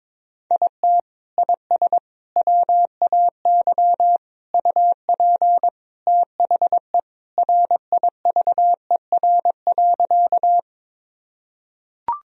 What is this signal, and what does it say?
Morse code, heard as a keyed tone, about 22 wpm, 700 Hz. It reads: IT IS WAY UP THE RIVER.
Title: Sentences from Top 1000 Words - Review Part 1 - 22wpm